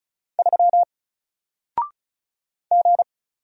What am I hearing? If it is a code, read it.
Morse code, heard as a keyed tone, about 36 wpm, 700 Hz. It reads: 3 G
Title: Single Character (Letter:Number:Punctuation:Prosign) Alt 35wpm